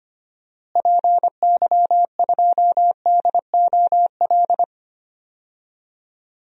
Morse code, heard as a keyed tone, about 25 wpm, 700 Hz. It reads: PY2DOL
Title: DXCC-100 Call Signs - 6 Characters - Repeated 2x - 25wpm